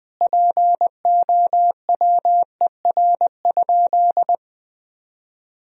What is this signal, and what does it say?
Morse code, heard as a keyed tone, about 20 wpm, 700 Hz. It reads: POWER?
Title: Sentences from Top 1000 Words - Review Part 2 - 20wpm